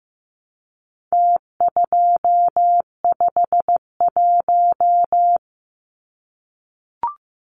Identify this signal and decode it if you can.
Morse code, heard as a keyed tone, about 15 wpm, 700 Hz. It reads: T251
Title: Numbers - Contest Style - 4-Digit 15wpm